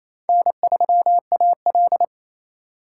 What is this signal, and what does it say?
Morse code, heard as a keyed tone, about 28 wpm, 700 Hz. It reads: N3AL
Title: USA Call Signs - 4 Characters - 3x Character Spacing - 28wpm